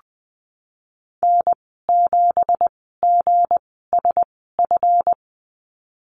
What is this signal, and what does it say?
Morse code, heard as a keyed tone, about 19 wpm, spaced as for 14 wpm, 700 Hz. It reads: N7GSF
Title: 167 Lesson 27g - Number 7 - Call Signs 20wpm